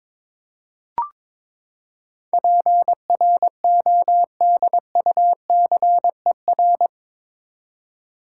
Morse code, heard as a keyed tone, about 21 wpm, 700 Hz. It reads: PRODUCER
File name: practice / 22wpm / Eight Letter Words - 22wpm